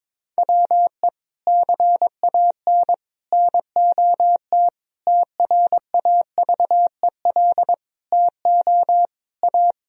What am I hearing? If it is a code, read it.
Morse code, heard as a keyed tone, about 22 wpm, 700 Hz. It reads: WE CAN NOT TRAVEL TO A
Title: Sentences from Top 1000 Words - Review Part 2 - 22wpm